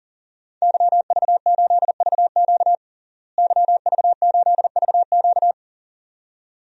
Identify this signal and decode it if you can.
Morse code, heard as a keyed tone, about 40 wpm, 700 Hz. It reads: YV8VQ YV8VQ
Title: DXCC-100 Call Signs - 5 Characters - Repeated 2x - 40wpm